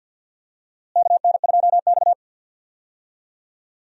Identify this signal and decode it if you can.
Morse code, heard as a keyed tone, about 46 wpm, 700 Hz. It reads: KN2X
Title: USA Call Signs - 4 Characters - Repeated 3x - 50wpm